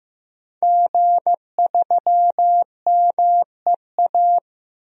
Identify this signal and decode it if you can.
Morse code, heard as a keyed tone, about 15 wpm, 700 Hz. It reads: G3MEA
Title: DXCC-100 Call Signs - 5 Characters - Repeated 2x - 15wpm